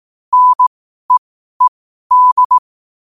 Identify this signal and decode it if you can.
Morse code, heard as a keyed tone, about 18 wpm, spaced as for 12 wpm, 1000 Hz. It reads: NEED
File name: qotdfiles / 12wpm